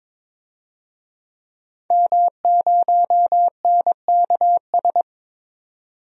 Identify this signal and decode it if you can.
Morse code, heard as a keyed tone, about 22 wpm, 700 Hz. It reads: M0NKS
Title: DXCC-100 Call Signs - 5 Characters - 22wpm